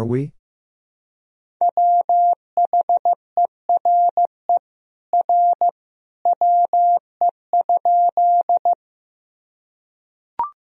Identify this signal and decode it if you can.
Morse code, heard as a keyed tone, about 15 wpm, 700 Hz. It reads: WHERE R WE?
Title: Sentences from Top 200 Words and Abbreviations 15wpm